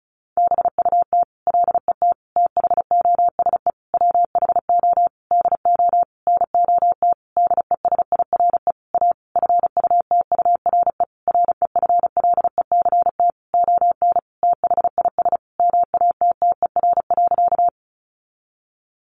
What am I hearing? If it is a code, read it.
Morse code, heard as a keyed tone, about 34 wpm, 700 Hz. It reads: BUT LET THOSE WHO DO NOT DESIRE A FUTURE REFLECT ON THIS MATTER.
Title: Sentences from Top 1500 Words 35wpm